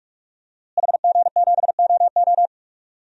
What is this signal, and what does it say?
Morse code, heard as a keyed tone, about 46 wpm, 700 Hz. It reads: HG7OO
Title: DXCC-100 Call Signs - 5 Characters - Repeated 3x - 45wpm